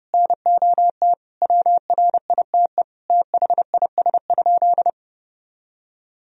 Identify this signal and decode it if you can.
Morse code, heard as a keyed tone, about 32 wpm, 700 Hz. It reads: NOT WRITE THIS?
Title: Sentences from Top 1000 Words - Review Part 1 - 30wpm